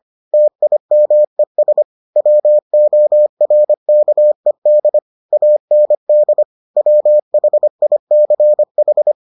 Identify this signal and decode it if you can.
Morse code, heard as a keyed tone, about 25 wpm, 600 Hz. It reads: TIMES WORKED AND WHICH